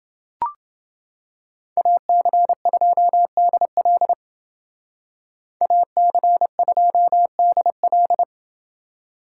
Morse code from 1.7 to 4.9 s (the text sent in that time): AC2DL